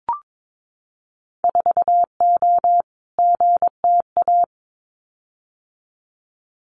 Morse code, heard as a keyed tone, about 22 wpm, 700 Hz. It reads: 4O GTA
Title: Winter Field Day Exchanges 22wpm